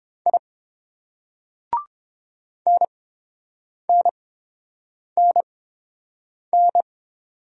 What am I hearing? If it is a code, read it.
Morse code, heard as a keyed tone, about 30 wpm, spaced as for 7 wpm, 700 Hz. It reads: I NNNN